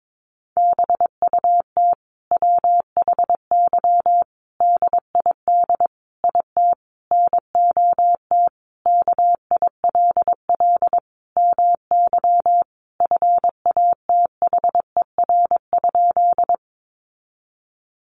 Morse code, heard as a keyed tone, about 22 wpm, 700 Hz. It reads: BUT WHY DID IT NOT KILL MY FATHER?
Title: Sentences from Top 1400 Words 22wpm